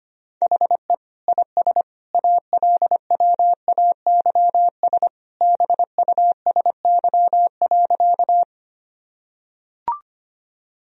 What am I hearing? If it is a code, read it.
Morse code, heard as a keyed tone, about 25 wpm, 700 Hz. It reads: HE IS ALWAYS BUSY.
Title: Sentences from Top 1000 Words - Review Part 3 - 25wpm